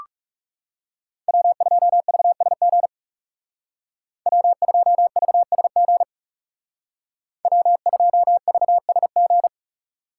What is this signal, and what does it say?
Morse code, heard as a keyed tone, about 37 wpm, 700 Hz. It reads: W2VSG W2VSG W2VSG
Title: Speed Racing - USA Call Signs 30wpm